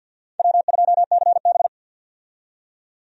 Morse code, heard as a keyed tone, about 48 wpm, 700 Hz. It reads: W2CB
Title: USA Call Signs - 4 Characters - 50wpm